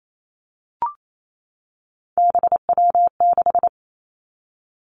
Morse code, heard as a keyed tone, about 28 wpm, 700 Hz. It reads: BW6
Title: Three Challenging Letter-Number Combinations 28wpm